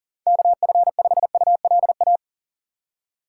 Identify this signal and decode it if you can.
Morse code, heard as a keyed tone, about 40 wpm, 700 Hz. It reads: KF5ULA
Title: USA Call Signs - 6 Characters - 3x Character Spacing - 40wpm